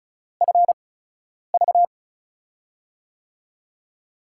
Morse code, heard as a keyed tone, about 36 wpm, spaced as for 10 wpm, 700 Hz. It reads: FV
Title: USA Call Signs - 5 Characters - 8x Character Spacing - 35wpm